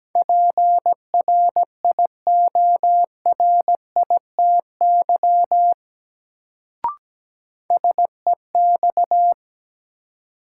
Morse code, heard as a keyed tone, about 17 wpm, 700 Hz. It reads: PRIORITY SEX
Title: Sentences from Top 1600 Words - New Word Drill Alt - 17wpm